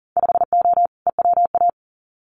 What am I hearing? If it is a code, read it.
Morse code, heard as a keyed tone, about 40 wpm, 700 Hz. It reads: 5O EWA
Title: Winter Field Day Exchanges Encore 40wpm